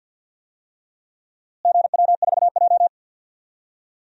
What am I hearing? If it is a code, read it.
Morse code, heard as a keyed tone, about 49 wpm, 700 Hz. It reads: GW4J
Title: DXCC-100 Call Signs - 4 Characters - Repeated 2x - 50wpm